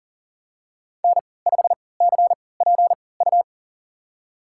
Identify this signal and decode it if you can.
Morse code, heard as a keyed tone, about 40 wpm, spaced as for 23 wpm, 700 Hz. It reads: N5CPU
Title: USA Call Signs - 5 Characters - 3x Character Spacing - 40wpm